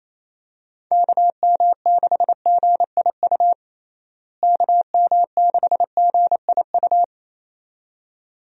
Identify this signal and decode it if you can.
Morse code, heard as a keyed tone, about 28 wpm, 700 Hz. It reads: KM6GIU KM6GIU
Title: USA Call Signs - 6 Characters - Repeated 2x - 28wpm